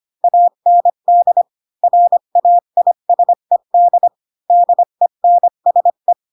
Morse code, heard as a keyed tone, about 25 wpm, 700 Hz. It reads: AND RAISED DENSE
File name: Pinocchio / Pinocchio ch6 20 25 ik1hge